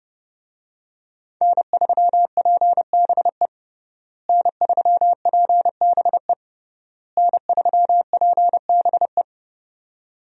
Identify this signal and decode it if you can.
Morse code, heard as a keyed tone, about 30 wpm, 700 Hz. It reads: N3PBE N3PBE N3PBE
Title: USA Call Signs - 5 Characters - Repeated 3x - 30wpm